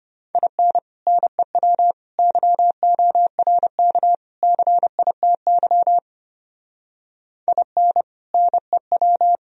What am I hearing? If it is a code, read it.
Morse code, heard as a keyed tone, about 30 wpm, 700 Hz. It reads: IN NEW YORK CITY IN NEW
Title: Speed-Racing - Sets of 4 Words 25wpm